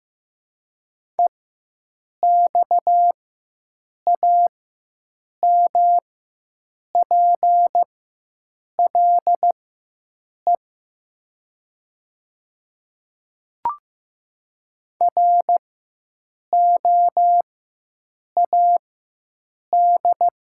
Morse code, heard as a keyed tone, about 15 wpm, spaced as for 7 wpm, 700 Hz. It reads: EXAMPLE ROAD
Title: 4x Character Spacing - Top 500 Words - 15wpm